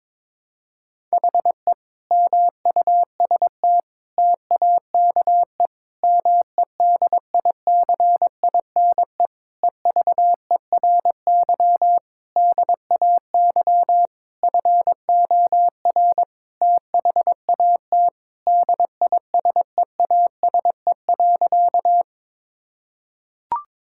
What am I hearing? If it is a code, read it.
Morse code, heard as a keyed tone, about 22 wpm, 700 Hz. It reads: HE MUST TAKE MEDICINE EVERY DAY FOR THAT DISEASE.
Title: Sentences from Top 1000 Words - Review Part 6 - 22wpm